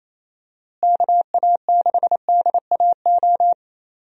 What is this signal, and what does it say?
Morse code, heard as a keyed tone, about 28 wpm, 700 Hz. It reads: KA6DAO
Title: USA Call Signs - 6 Characters - Repeated 2x - 28wpm